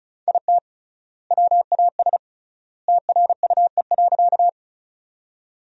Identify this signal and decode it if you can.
Morse code, heard as a keyed tone, about 36 wpm, 700 Hz. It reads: IT WAS TRUE.
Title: 3x Word-Spacing - Sentences from Top 400 Words - 35wpm